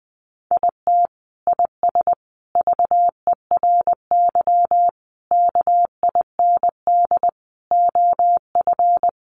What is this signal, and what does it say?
Morse code, heard as a keyed tone, about 20 wpm, 700 Hz. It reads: IT IS VERY KIND OF
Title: Sentences from Top 200 Words 20wpm